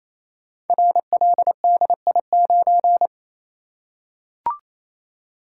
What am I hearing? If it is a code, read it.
Morse code, heard as a keyed tone, about 27 wpm, 700 Hz. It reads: RLDI9
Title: Code Groups - 5 Letter:Numbers - 28wpm